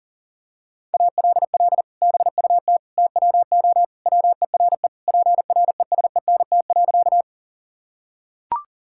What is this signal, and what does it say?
Morse code, heard as a keyed tone, about 40 wpm, 700 Hz. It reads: ALL BUT TWO WERE PRESENT.
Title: Sentences from Top 1000 Words - Review Part 5 - 40wpm